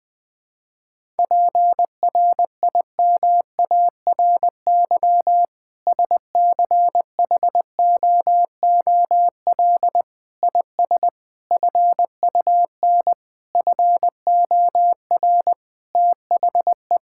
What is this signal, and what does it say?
Morse code, heard as a keyed tone, about 20 wpm, 700 Hz. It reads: PRIMARY SCHOOL IS FUN FOR THE
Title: Sentences from Top 1000 Words - Review Part 5 - 20wpm